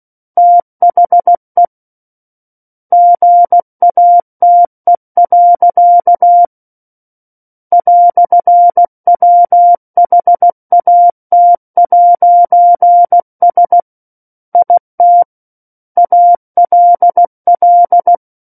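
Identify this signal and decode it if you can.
Morse code, heard as a keyed tone, about 16 wpm, 700 Hz. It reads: THE GATE. "WHAT'S IT ALL
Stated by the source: Morse practice of book text at 16 wpm.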